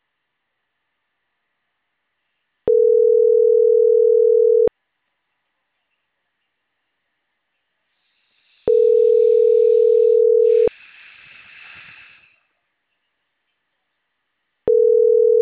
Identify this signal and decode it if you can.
Ringback tone (North America), 440 + 480 Hz.